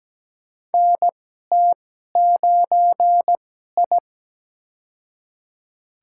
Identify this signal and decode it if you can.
Morse code, heard as a keyed tone, about 17 wpm, spaced as for 12 wpm, 700 Hz. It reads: NT9I
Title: USA Call Signs - 2x Character Spacing - 17wpm